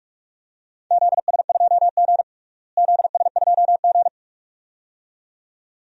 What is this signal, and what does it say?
Morse code, heard as a keyed tone, about 46 wpm, 700 Hz. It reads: ZS2G ZS2G
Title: DXCC-100 Call Signs - 4 Characters - Repeated 2x - 45wpm